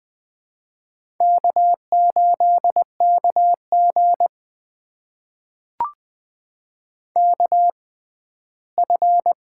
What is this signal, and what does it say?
Morse code, heard as a keyed tone, about 20 wpm, spaced as for 14 wpm, 700 Hz. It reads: K8KG KF
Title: USA Call Signs - 4 Characters - 6x Character Spacing - 20wpm